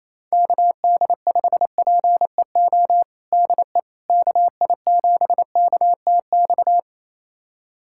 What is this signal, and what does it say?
Morse code, heard as a keyed tone, about 28 wpm, 700 Hz. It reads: KD5PEO DE KI7KTX
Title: USA Callsign DE Callsign - 6 Characters - 28wpm